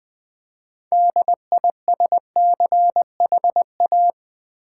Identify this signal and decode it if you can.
Morse code, heard as a keyed tone, about 20 wpm, 700 Hz. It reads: DISCHA
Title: Build A Word - 3 Letters At A Time - 9-Letter Words - 20wpm